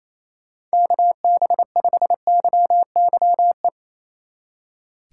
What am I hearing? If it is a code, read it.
Morse code, heard as a keyed tone, about 35 wpm, 700 Hz. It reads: KB5YYE
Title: Speed Racing - USA Call Signs 28wpm